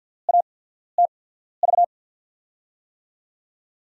Morse code, heard as a keyed tone, about 49 wpm, spaced as for 14 wpm, 700 Hz. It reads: ATV
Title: USA Call Signs - 5 Characters - 8x Character Spacing - 50wpm